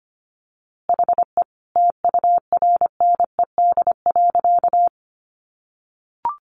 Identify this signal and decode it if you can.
Morse code, heard as a keyed tone, about 25 wpm, 700 Hz. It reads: HE TURNED.
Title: A Sound Of Thunder 25wpm